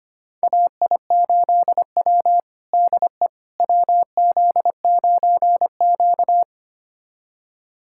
Morse code, heard as a keyed tone, about 25 wpm, 700 Hz. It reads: AI8W DE WZ9Q
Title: USA Callsign DE Callsign - 4 Characters - 25wpm